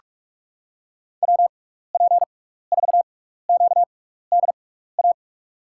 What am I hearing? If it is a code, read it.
Morse code, heard as a keyed tone, about 43 wpm, spaced as for 15 wpm, 700 Hz. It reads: WP4QDA
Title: USA Call Signs - 6x Character Spacing - 45wpm